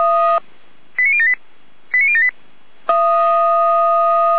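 Five-tone selective call (ZVEI), heard as quick runs of five tones, tones of about 70 ms. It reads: 89078 89078
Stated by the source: ZVEI five-tone selective call received by SDR, from the Feuerwehr Orlamünde audio server.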